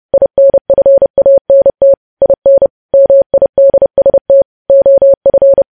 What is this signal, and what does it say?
Morse code, heard as a keyed tone, about 30 wpm, 550 Hz. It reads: INFANT IN MIDST OF
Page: news headlines in morse code at 30 wpm